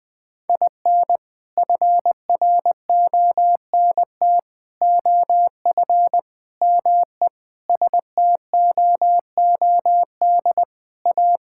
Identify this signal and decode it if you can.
Morse code, heard as a keyed tone, about 20 wpm, 700 Hz. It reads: IN FRONT OF ME STOOD A
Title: Sentences with 4-Word Phrases - Spoken Beginning - 20wpm